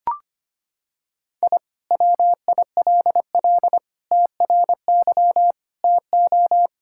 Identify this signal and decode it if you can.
Morse code, heard as a keyed tone, about 25 wpm, 700 Hz. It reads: I WILL TRY TO
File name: practice / 25wpm / Sentences with 3-Word Phrases - 25wpm